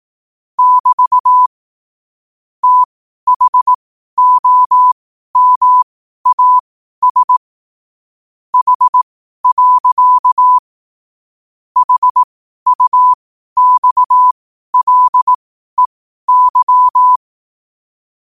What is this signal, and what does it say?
Morse code, heard as a keyed tone, about 18 wpm, spaced as for 12 wpm, 1000 Hz. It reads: <BT> THOMAS H. HUXLEY
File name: qotdfiles / 12wpm